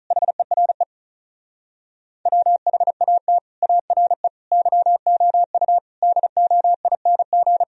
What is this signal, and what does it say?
Morse code, heard as a keyed tone, about 46 wpm, 700 Hz. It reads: HERE WHAT ARE YOU DOING